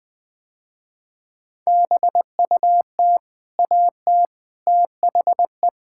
Morse code, heard as a keyed tone, about 20 wpm, 700 Hz. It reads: BUT AT THE
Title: Sentences with 4-Word Phrases - Spoken Beginning - 20wpm